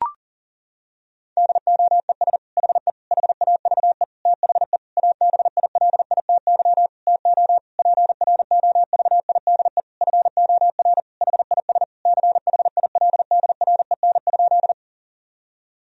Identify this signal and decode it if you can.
Morse code, heard as a keyed tone, about 40 wpm, 700 Hz. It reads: DOES HE HAVE THE ABILITY TO PROVIDE FOR HIS CHILDREN?
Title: Sentences from Top 1000 Words - Review Part 3 - 40wpm